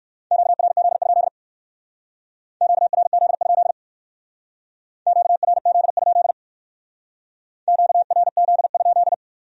Morse code, heard as a keyed tone, about 58 wpm, 700 Hz. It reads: QRZ? QRZ? QRZ? QRZ?